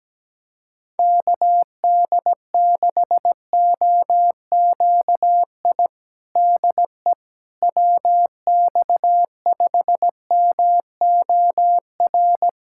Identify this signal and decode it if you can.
Morse code, heard as a keyed tone, about 17 wpm, 700 Hz. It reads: KD6OQI DE WX5MOR
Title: USA Callsign DE Callsign - 6 Characters - 17wpm